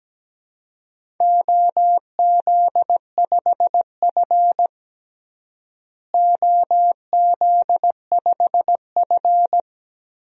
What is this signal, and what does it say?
Morse code, heard as a keyed tone, about 17 wpm, 700 Hz. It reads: OZ5F OZ5F
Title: DXCC-100 Call Signs - 4 Characters - Repeated 3x - 17wpm